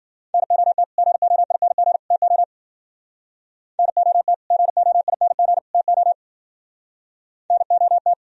Morse code, heard as a keyed tone, about 51 wpm, 700 Hz. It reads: NOT GOING TO NOT GOING TO NOT